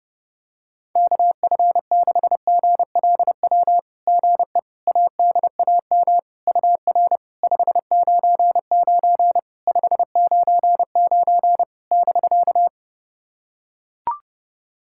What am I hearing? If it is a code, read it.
Morse code, heard as a keyed tone, about 30 wpm, 700 Hz. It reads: KF6GLW GE ADAM UR 599 599 <BK>
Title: SOTA QSO Practice 30wpm